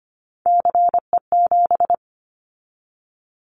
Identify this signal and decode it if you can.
Morse code, heard as a keyed tone, about 25 wpm, 700 Hz. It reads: CE7
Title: Callsign Prefixes for Top 100 DXCC - 25wpm